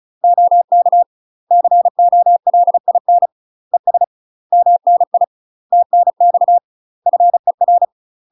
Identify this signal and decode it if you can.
Morse code, heard as a keyed tone, about 34 wpm, 700 Hz. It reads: OK COLIN ES MNI TNX FER
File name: UK MASTER QSOs TRAINING / 35WPM / UK MASTER QSOs 0001